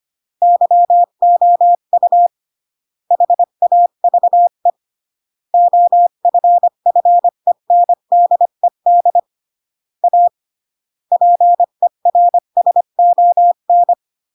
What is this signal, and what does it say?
Morse code, heard as a keyed tone, about 25 wpm, 700 Hz. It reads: YOU HAVE OFFENDED A PERSON